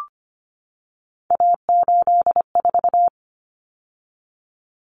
Morse code, heard as a keyed tone, about 25 wpm, 700 Hz. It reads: A84
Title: Common Cut Numbers - 3-Digit 25wpm